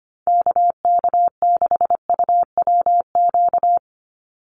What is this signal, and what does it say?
Morse code, heard as a keyed tone, about 25 wpm, 700 Hz. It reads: KK6UWQ
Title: USA Call Signs 25wpm